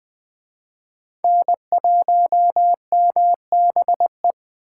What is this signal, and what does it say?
Morse code, heard as a keyed tone, about 20 wpm, 700 Hz. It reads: N1MBE
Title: Speed Racing - USA 1x3 Call Signs 17wpm